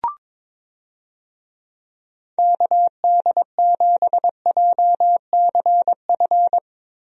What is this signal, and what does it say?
Morse code, heard as a keyed tone, about 22 wpm, 700 Hz. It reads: KD7JCF
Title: USA Call Signs - 6 Characters - Repeated 2x - 22wpm